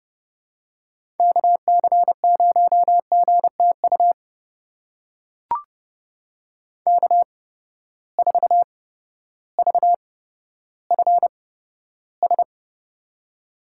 Morse code, heard as a keyed tone, about 30 wpm, spaced as for 14 wpm, 700 Hz. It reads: KC0GTU K4VFS
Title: USA Call Signs - 8x Character Spacing - 30wpm